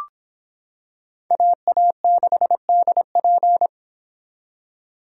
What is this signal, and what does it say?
Morse code, heard as a keyed tone, about 26 wpm, 700 Hz. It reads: AA6DP